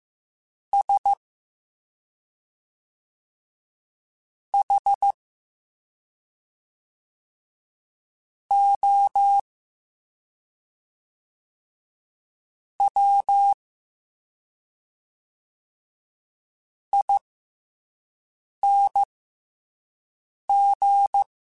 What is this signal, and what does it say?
Morse code, heard as a keyed tone, about 15 wpm, spaced as for 3 wpm, 800 Hz. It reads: SHOWING